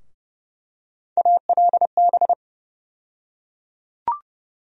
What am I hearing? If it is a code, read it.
Morse code, heard as a keyed tone, about 30 wpm, 700 Hz. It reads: ALB
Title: USA Call Sign Postfixes - 3 Letters 30wpm